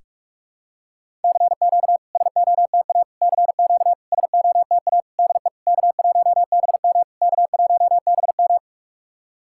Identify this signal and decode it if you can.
Morse code, heard as a keyed tone, about 46 wpm, 690 Hz. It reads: CQ SOTA CQ SOTA DE K1BM K1BM
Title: SOTA QSO Practice 45wpm